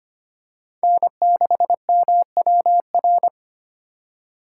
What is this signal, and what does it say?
Morse code, heard as a keyed tone, about 25 wpm, 700 Hz. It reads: N6MWR
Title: USA Call Signs - 5 Characters - 3x Character Spacing - 25wpm